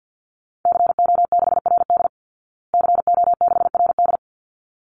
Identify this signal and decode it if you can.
Morse code, heard as a keyed tone, about 49 wpm, 700 Hz. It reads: CO6RD CO6RD
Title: Call Signs Repeated 3x - 50wpm